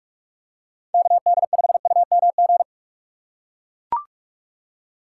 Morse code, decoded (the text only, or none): KD5UMG